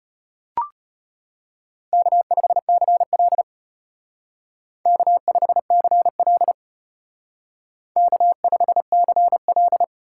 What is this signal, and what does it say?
Morse code, heard as a keyed tone, about 32 wpm, 700 Hz. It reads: K5CL K5CL K5CL